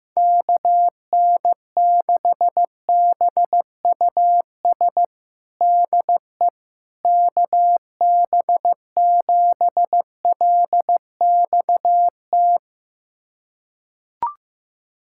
Morse code, heard as a keyed tone, about 15 wpm, 700 Hz. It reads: KN6BUS DE KB7LXT
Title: USA Callsign DE Callsign - 6 Characters - 15wpm